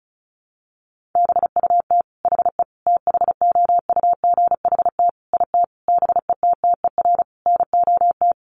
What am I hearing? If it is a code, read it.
Morse code, heard as a keyed tone, about 34 wpm, 700 Hz. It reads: BUT HE THOUGHT IT BETTER NOT
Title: Sentences from Top 600 Words 35wpm